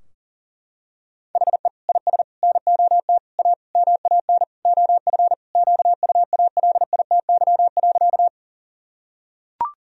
Morse code, heard as a keyed tone, about 37 wpm, 700 Hz. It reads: HE IS NOT A MAN OF QUALITY.
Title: Sentences from Top 1000 Words - Review Part 4 - 40wpm